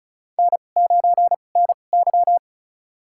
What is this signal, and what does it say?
Morse code, heard as a keyed tone, about 36 wpm, spaced as for 25 wpm, 700 Hz. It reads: N9NY
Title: USA Call Signs - 4 Characters - 2x Character Spacing - 35wpm